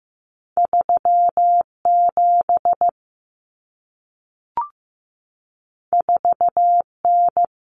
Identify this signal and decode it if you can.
Morse code, heard as a keyed tone, about 15 wpm, 700 Hz. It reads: 37 4N
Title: Common Cut Numbers - 2-Digit 15wpm